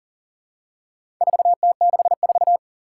Morse code, heard as a keyed tone, about 40 wpm, 700 Hz. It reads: 4T64
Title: Four Challenging Letter-Number Combinations 40wpm